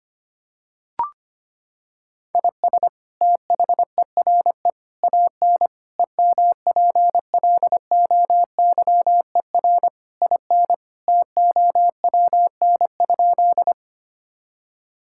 Morse code, heard as a keyed tone, about 25 wpm, 700 Hz. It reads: IS THERE AN EMPLOYER IN TOWN?
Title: Sentences from Top 1600 Words 25wpm